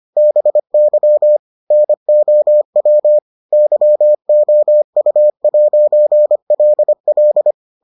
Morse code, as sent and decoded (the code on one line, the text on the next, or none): -... -.-- / -. --- .-- / -.-- --- ..- .----. .-.. .-..
BY NOW YOU'LL